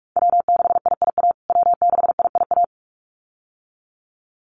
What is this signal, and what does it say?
Morse code, heard as a keyed tone, about 44 wpm, 700 Hz. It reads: W6IIA W6IIA
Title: USA Call Signs Repeated 2x 45wpm